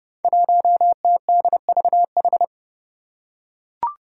Morse code, heard as a keyed tone, about 30 wpm, 700 Hz. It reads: 1TDVH
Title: Five Challenging Letter-Number Combinations 30wpm